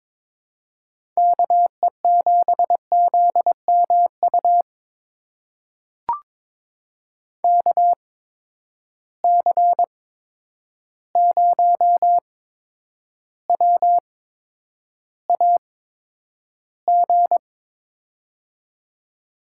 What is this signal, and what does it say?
Morse code, heard as a keyed tone, about 22 wpm, spaced as for 9 wpm, 700 Hz. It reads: KE7ZMU KC0WAG
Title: USA Call Signs - 8x Character Spacing - 22wpm